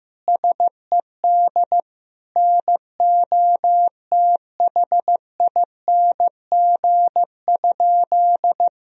Morse code, heard as a keyed tone, about 15 wpm, 700 Hz. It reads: SED NOTHING?
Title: Sentences from Top 500 Words and Abbreviations 15wpm